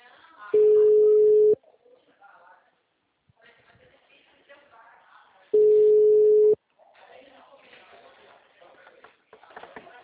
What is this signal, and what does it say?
Ringback tone (Europe, 425 Hz), with speech.